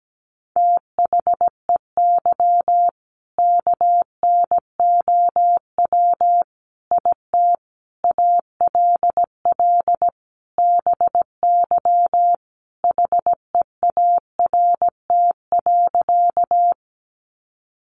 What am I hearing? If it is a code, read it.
Morse code, heard as a keyed tone, about 17 wpm, 700 Hz. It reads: THEY KNOW IT ALL BY HEART.
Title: Sentences from Top 1200 Words 17wpm